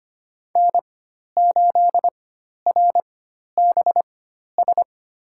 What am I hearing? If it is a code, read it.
Morse code, heard as a keyed tone, about 25 wpm, spaced as for 12 wpm, 700 Hz. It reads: N8RBS
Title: USA Call Signs - 4x Character Spacing - 25wpm